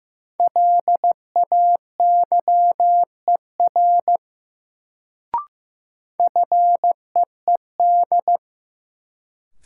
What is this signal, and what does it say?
Morse code, heard as a keyed tone, about 15 wpm, 700 Hz. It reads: LAYER FEED
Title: Sentences from Top 1600 Words - New Word Drill - 15wpm